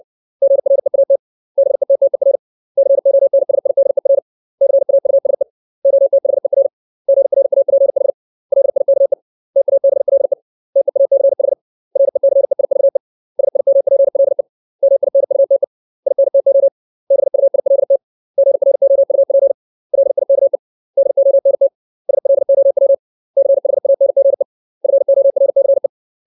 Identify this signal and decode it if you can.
Morse code, heard as a keyed tone, about 60 wpm, 550 Hz. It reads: GREAT BETTER CONSIDER CAUSE OTHER GRAPH LIKE TABLE TEACH RECEIVE SIMPLE MINUTE INTO BUILT GROUP LIKE DONT SLOW CHANGE FORCE
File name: 632 words / 632 Words 60WPM